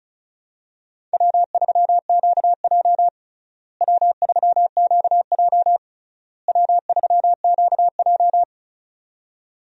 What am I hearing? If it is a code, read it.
Morse code, heard as a keyed tone, about 36 wpm, 700 Hz. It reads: W3QJ W3QJ W3QJ
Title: USA Call Signs - 4 Characters - Repeated 3x - 35wpm